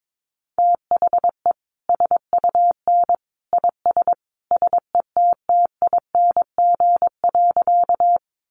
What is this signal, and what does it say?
Morse code, heard as a keyed tone, about 22 wpm, 700 Hz. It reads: THE SUN IS SETTING.
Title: Sentences from Top 1300 Words 22wpm